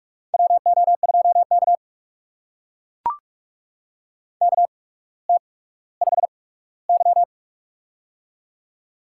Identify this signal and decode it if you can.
Morse code, heard as a keyed tone, about 44 wpm, spaced as for 19 wpm, 700 Hz. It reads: WO2K KT5Y
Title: USA Call Signs - 8x Character Spacing - 45wpm